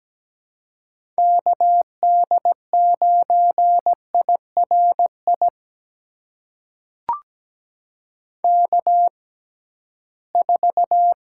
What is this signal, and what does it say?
Morse code, heard as a keyed tone, about 17 wpm, spaced as for 13 wpm, 700 Hz. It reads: KD9IRI K4
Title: USA Call Signs - 6x Character Spacing - 17wpm